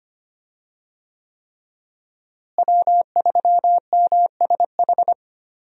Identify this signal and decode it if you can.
Morse code, heard as a keyed tone, about 25 wpm, 700 Hz. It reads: W3MSH
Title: USA Call Signs - 5 Characters - Repeated 3x - 25wpm